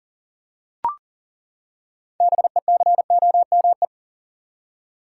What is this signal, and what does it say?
Morse code, heard as a keyed tone, about 40 wpm, 700 Hz. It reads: BECOME
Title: Six Letter Words - 40wpm